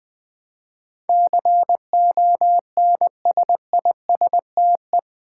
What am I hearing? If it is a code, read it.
Morse code, heard as a keyed tone, about 21 wpm, 700 Hz. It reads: CONSISTE
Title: Build A Word - 4 Letters At A Time - 10-Letter Words - 20wpm